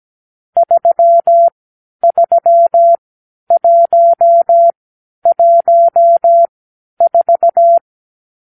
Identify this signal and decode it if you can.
Morse code, heard as a keyed tone, about 17 wpm, spaced as for 10 wpm, 680 Hz. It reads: 33114
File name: k7qo / Lesson 46 (1-4)